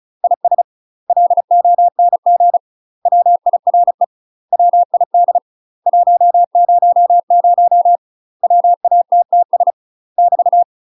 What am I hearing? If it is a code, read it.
Morse code, heard as a keyed tone, about 35 wpm, 700 Hz. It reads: IS LONG WIRE WID 100 WATTS <BT>
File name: UK MASTER QSOs TRAINING / 35WPM / UK MASTER QSOs 0043